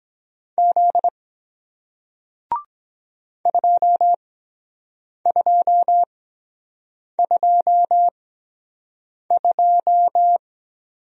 Morse code, heard as a keyed tone, about 24 wpm, 700 Hz. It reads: Z 2 2 2 2